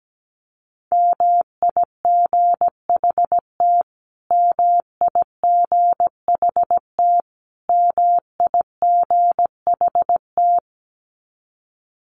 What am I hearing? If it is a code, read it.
Morse code, heard as a keyed tone, about 17 wpm, 700 Hz. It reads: MIGHT MIGHT MIGHT
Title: Top 300 Words Repeated 17wpm